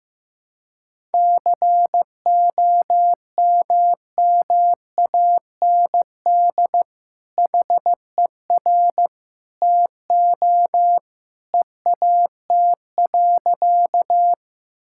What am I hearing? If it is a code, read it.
Morse code, heard as a keyed tone, about 15 wpm, 700 Hz. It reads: COMMAND HER TO EAT.
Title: Sentences from Top 1600 Words 15wpm